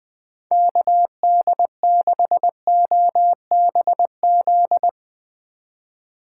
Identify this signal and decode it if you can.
Morse code, heard as a keyed tone, about 20 wpm, 700 Hz. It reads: KD6OBZ
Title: USA Call Signs - 6 Characters - 3x Character Spacing - 20wpm